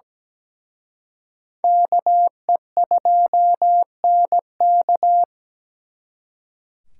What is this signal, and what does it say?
Morse code, heard as a keyed tone, about 20 wpm, 700 Hz. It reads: KE2NK
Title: Speed Racing - USA 2x2 Call Signs 17wpm